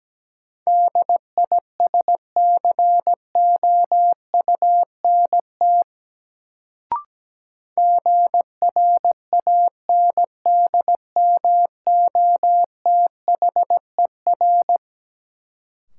Morse code, heard as a keyed tone, about 17 wpm, 700 Hz. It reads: DISCOUNT GRANDMOTHER
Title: Sentences from Top 1200 Words - New Word Drill - 17wpm